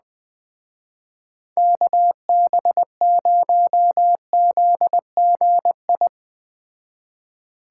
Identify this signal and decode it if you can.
Morse code, heard as a keyed tone, about 20 wpm, 700 Hz. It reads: KB0ZGI
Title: Speed Racing - USA 2x3 Call Signs 17wpm